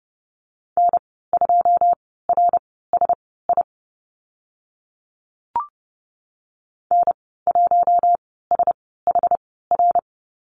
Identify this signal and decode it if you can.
Morse code, heard as a keyed tone, about 31 wpm, spaced as for 17 wpm, 700 Hz. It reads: N2RSI N1SHR
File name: Lesson 14g - Number 2 - Call Signs 30wpm